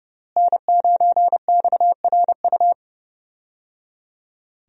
Morse code, heard as a keyed tone, about 30 wpm, 700 Hz. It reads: N9XRU
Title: USA Call Signs - 5 Characters - Repeated 3x - 30wpm